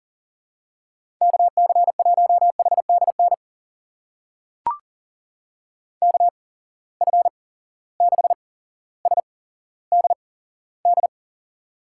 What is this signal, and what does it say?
Morse code, heard as a keyed tone, about 40 wpm, spaced as for 17 wpm, 700 Hz. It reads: KC1HDN KF6SDD
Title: USA Call Signs - 6 Characters - 8x Character Spacing - 40wpm